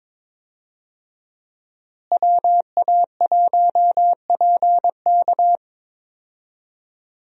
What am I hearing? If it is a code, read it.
Morse code, heard as a keyed tone, about 22 wpm, 700 Hz. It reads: WA1PK
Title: USA Call Signs - 5 Characters - Repeated 2x - 22wpm